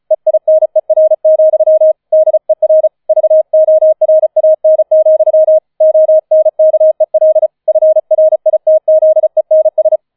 Morse code, heard as a keyed tone, about 35 wpm, 630 Hz. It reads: EINER, DER VORAN, ONKEL FRITZENS
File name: 005 Fünfter Streich 35wpm